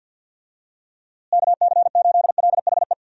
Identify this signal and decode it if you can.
Morse code, heard as a keyed tone, about 52 wpm, 700 Hz. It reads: KC8LHE
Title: USA Call Signs - 6 Characters - Repeated 2x - 50wpm